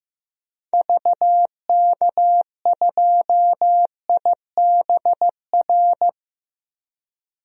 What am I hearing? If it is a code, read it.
Morse code, heard as a keyed tone, about 16 wpm, 700 Hz. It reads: VK2IBR
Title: DXCC-100 Call Signs - 6 Characters - 15wpm